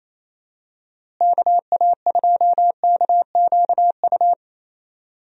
Morse code, heard as a keyed tone, about 28 wpm, 700 Hz. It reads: KA2KQU
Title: USA Call Signs - 6 Characters - Repeated 2x - 28wpm